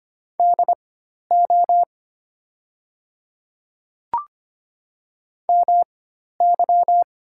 Morse code, heard as a keyed tone, about 25 wpm, spaced as for 12 wpm, 700 Hz. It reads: DO MY
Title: 4x Character Spacing - Top 100 Words - 25wpm